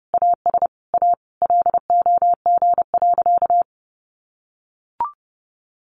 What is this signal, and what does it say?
Morse code, heard as a keyed tone, about 30 wpm, 700 Hz. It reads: AS A LOG.
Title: Sentences from Top 1500 Words 30wpm